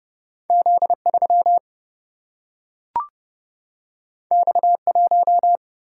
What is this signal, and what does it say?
Morse code, heard as a keyed tone, about 30 wpm, 700 Hz. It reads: Z 3 X 1
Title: Single-Letter Single-Number 30wpm